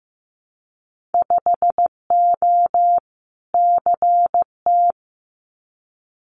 Morse code, heard as a keyed tone, about 15 wpm, 700 Hz. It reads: 5O CT
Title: Winter Field Day Exchanges Encore 15wpm